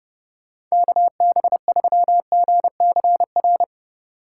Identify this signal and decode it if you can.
Morse code, heard as a keyed tone, about 32 wpm, 700 Hz. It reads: KB3GCR